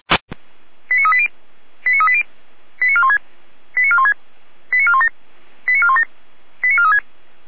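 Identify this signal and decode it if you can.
Five-tone selective call (ZVEI), heard as quick runs of five tones, tones of about 70 ms. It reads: 89380 89380 89416 89416 89417 89417 89436
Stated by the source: Received on an SDR.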